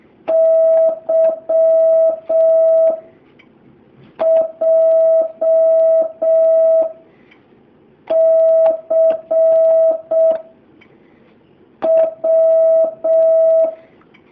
Morse code, heard as a keyed tone, about 6 wpm, 650 Hz. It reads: Y J C W
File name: jh8cbh / 2019cw